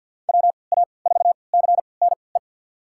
Morse code, heard as a keyed tone, about 48 wpm, spaced as for 28 wpm, 700 Hz. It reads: WA4CNE
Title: USA Call Signs - 3x Character Spacing - 50wpm